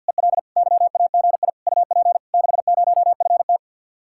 Morse code, heard as a keyed tone, about 64 wpm, 700 Hz. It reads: EL YAGI UP 60FT